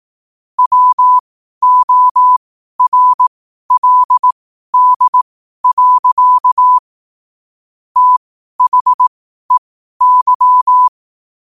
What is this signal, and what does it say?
Morse code, heard as a keyed tone, about 18 wpm, spaced as for 12 wpm, 1000 Hz. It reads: WORLD. THEY
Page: Quotes for Fri, 15 Aug 2025 in Morse Code at 12 words per minute.